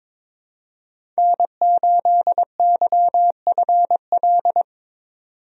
Morse code, heard as a keyed tone, about 22 wpm, 700 Hz. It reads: N8YFL
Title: USA Call Signs - 5 Characters - 22wpm